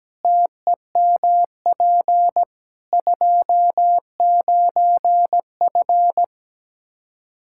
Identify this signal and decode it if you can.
Morse code, heard as a keyed tone, about 17 wpm, 700 Hz. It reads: TEMP 29F
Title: Practice QSO - Exchange 2 - Info - 17wpm